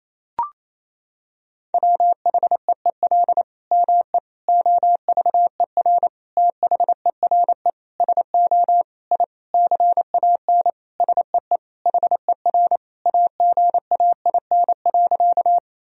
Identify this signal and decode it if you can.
Morse code, heard as a keyed tone, about 28 wpm, 700 Hz. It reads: WHEEL ME OVER THERE SO I CAN SEE HER AGAIN.
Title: Sentences from Top 1000 Words - Review Part 3 - 28wpm